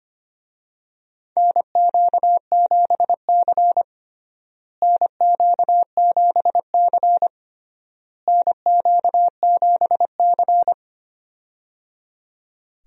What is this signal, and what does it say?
Morse code, heard as a keyed tone, about 25 wpm, 700 Hz. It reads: NQ7C NQ7C NQ7C
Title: USA Call Signs - 4 Characters - Repeated 3x - 25wpm